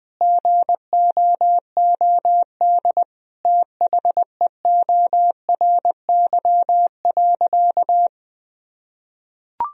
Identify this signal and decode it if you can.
Morse code, heard as a keyed tone, about 20 wpm, 700 Hz. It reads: GOOD THEORY.
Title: Sentences from Top 1000 Words - Review Part 3 - 20wpm